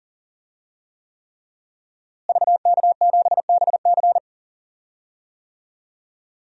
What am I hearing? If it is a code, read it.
Morse code, heard as a keyed tone, about 40 wpm, 700 Hz. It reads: VK7BC
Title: DXCC-100 Call Signs - 5 Characters - Repeated 3x - 40wpm